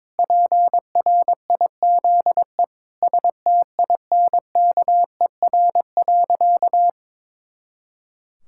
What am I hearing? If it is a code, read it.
Morse code, heard as a keyed tone, about 22 wpm, 700 Hz. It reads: PRIZE STINKER.
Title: Pangram Challenge v2 - 22wpm